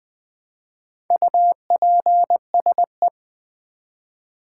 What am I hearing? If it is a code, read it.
Morse code, heard as a keyed tone, about 20 wpm, 700 Hz. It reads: UPSE
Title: Build A Word - 1 Letter At A Time - 5-Letter Words - 20wpm